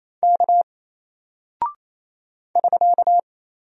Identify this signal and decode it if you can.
Morse code, heard as a keyed tone, about 28 wpm, 700 Hz. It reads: K <SK>
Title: Single Character (Letter:Number:Punctuation:Prosign) Alt 28wpm